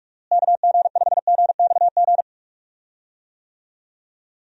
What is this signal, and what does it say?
Morse code, heard as a keyed tone, about 45 wpm, 700 Hz. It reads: KG5GXG
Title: USA Call Signs - 6 Characters - Repeated 2x - 45wpm